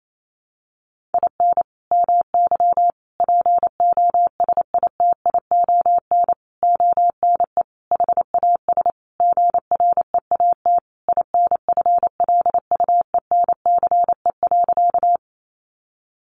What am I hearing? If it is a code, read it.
Morse code, heard as a keyed tone, about 28 wpm, 700 Hz. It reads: IN MY POSITION ONE HAS GREAT INFLUENCE.
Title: Sentences from Top 1100 Words 28wpm